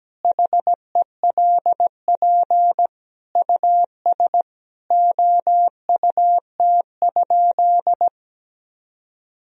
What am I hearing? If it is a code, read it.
Morse code, heard as a keyed tone, about 17 wpm, 700 Hz. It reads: HELP US OUT?
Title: Sentences from Top 1000 Words - Review Part 3 - 17wpm